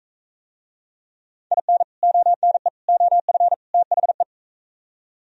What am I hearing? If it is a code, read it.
Morse code, heard as a keyed tone, about 44 wpm, 700 Hz. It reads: IN ONE OF THE